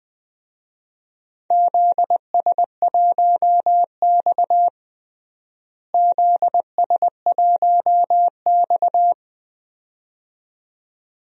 Morse code, heard as a keyed tone, about 21 wpm, 700 Hz. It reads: ZS1X ZS1X
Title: DXCC-100 Call Signs - 4 Characters - Repeated 2x - 20wpm